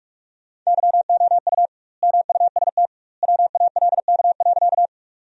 Morse code, heard as a keyed tone, about 46 wpm, 700 Hz. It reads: YOU MUST WALK.
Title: 2x Word-Spacing - Sentences from Top 300 Words - 45wpm